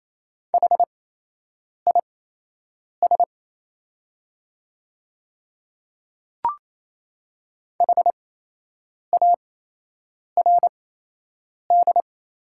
Morse code, heard as a keyed tone, about 28 wpm, spaced as for 8 wpm, 700 Hz. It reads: HIS HARD
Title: 8x Character Spacing - Top 500 Words - 28wpm